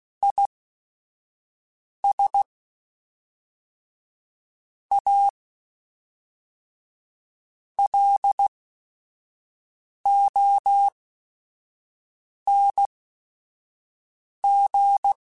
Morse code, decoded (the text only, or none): IS A LONG